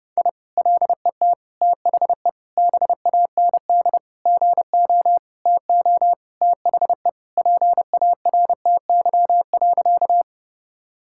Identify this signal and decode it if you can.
Morse code, heard as a keyed tone, about 32 wpm, 700 Hz. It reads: I LET THE BAND GO TO THE PARTY.
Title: Sentences from Top 1600 Words 30wpm